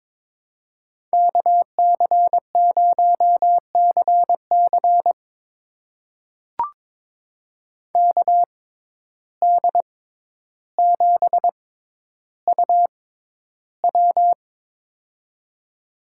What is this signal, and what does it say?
Morse code, heard as a keyed tone, about 22 wpm, spaced as for 11 wpm, 700 Hz. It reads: KC0CC KD7UW
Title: USA Call Signs - 5 Characters - 6x Character Spacing - 22wpm